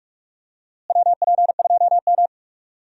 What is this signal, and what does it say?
Morse code, heard as a keyed tone, about 45 wpm, 700 Hz. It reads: WP2M
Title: USA Call Signs - 4 Characters - Repeated 3x - 45wpm